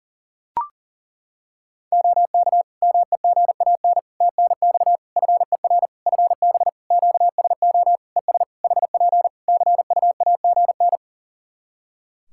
Morse code, heard as a keyed tone, about 37 wpm, 700 Hz. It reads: OK MEGAN TNX FER FB QSO ES HP CUAGN
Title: Practice QSO - Exchange 4 - Ending - 40wpm